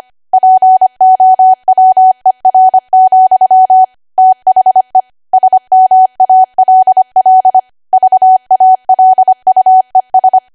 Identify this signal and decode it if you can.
Morse code, heard as a keyed tone, about 25 wpm, 750 Hz. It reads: POWER, THE SMALL VALUES
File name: cwa / W1AW-Level3 / 25WPM